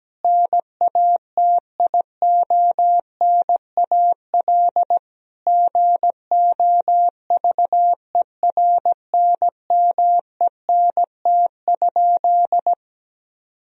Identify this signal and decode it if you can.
Morse code, heard as a keyed tone, about 17 wpm, 700 Hz. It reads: NATIONAL GOVERNMENT?
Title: Sentences from Top 1000 Words - Review Part 4 - 17wpm